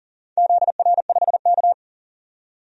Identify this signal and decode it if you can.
Morse code, heard as a keyed tone, about 38 wpm, 690 Hz. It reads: ZR5K
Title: DXCC-100 Call Signs - 4 Characters - 40wpm